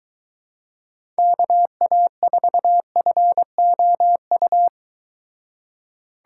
Morse code, heard as a keyed tone, about 23 wpm, 700 Hz. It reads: KA4FOU